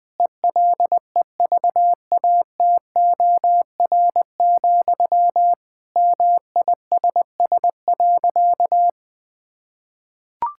Morse code, heard as a keyed tone, about 19 wpm, 700 Hz. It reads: ELEVATOR, MISS.
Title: Sentences from Top 1600 Words 20wpm